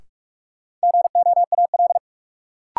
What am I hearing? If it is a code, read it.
Morse code, heard as a keyed tone, about 41 wpm, 700 Hz. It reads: GOAL
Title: Sentences from Top 1400 Words - New Word Drill Alt - 45wpm